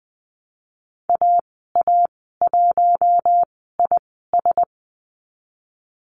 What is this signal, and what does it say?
Morse code, heard as a keyed tone, about 20 wpm, spaced as for 15 wpm, 700 Hz. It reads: AA1IS
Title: 042 Lesson 8g - Number 1 - Call Signs 20wpm